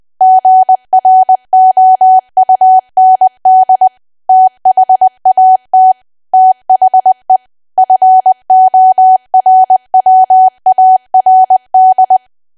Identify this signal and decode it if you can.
Morse code, heard as a keyed tone, about 20 wpm, 750 Hz. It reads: GROUND THAT THE FORWARD
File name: cwa / W1AW-Level3 / 20WPM